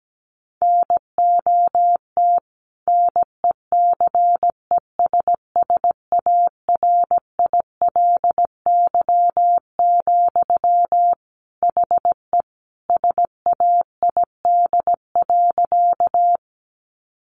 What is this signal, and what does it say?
Morse code, heard as a keyed tone, about 17 wpm, 700 Hz. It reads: NOT NECESSARILY, HE SAID.
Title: Sentences from Top 1400 Words 17wpm